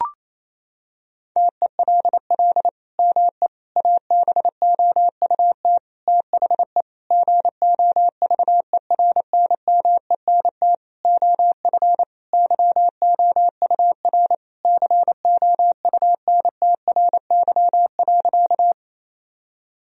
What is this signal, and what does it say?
Morse code, heard as a keyed tone, about 28 wpm, 700 Hz. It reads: TELL ME ABOUT THE GOVERNMENT OF YOUR COUNTRY.
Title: Sentences from Top 1000 Words - Review Part 3 - 28wpm